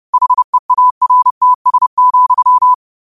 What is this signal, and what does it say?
Morse code, heard as a keyed tone, about 30 wpm, 1000 Hz. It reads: HEARTS,